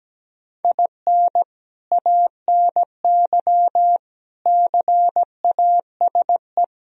Morse code, heard as a keyed tone, about 17 wpm, 700 Hz. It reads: IN ANY CASE
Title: Sentences from Top 1000 Words - Review Part 1 - 17wpm